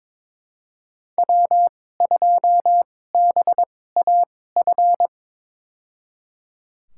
Morse code, heard as a keyed tone, about 22 wpm, spaced as for 16 wpm, 700 Hz. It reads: W2BAF
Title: USA Call Signs - 2x Character Spacing - 22wpm